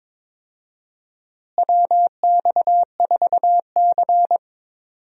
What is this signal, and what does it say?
Morse code, heard as a keyed tone, about 22 wpm, 700 Hz. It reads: WX4C
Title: USA Call Signs - 4 Characters - Repeated 2x - 22wpm